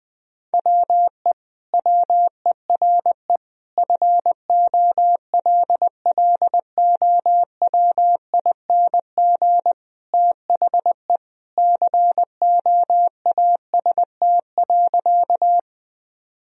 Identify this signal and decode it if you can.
Morse code, heard as a keyed tone, about 20 wpm, 700 Hz. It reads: WE WERE FOLLOWING THE COAST.
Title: Sentences from Top 1000 Words - Review Part 4 - 20wpm